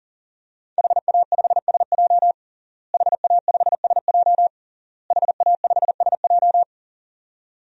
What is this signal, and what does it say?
Morse code, heard as a keyed tone, about 40 wpm, 700 Hz. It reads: HA5SJ HA5SJ HA5SJ
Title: DXCC-100 Call Signs - 5 Characters - Repeated 3x - 40wpm